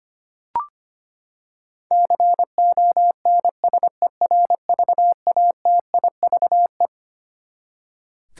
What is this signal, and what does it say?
Morse code, heard as a keyed tone, about 25 wpm, 700 Hz. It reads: CONSERVATIVE
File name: Twelve Letter Words - 25wpm